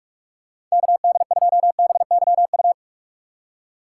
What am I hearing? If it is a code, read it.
Morse code, heard as a keyed tone, about 46 wpm, 700 Hz. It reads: KD2BYU
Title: USA Call Signs - 6 Characters - 4x Character Spacing - 45wpm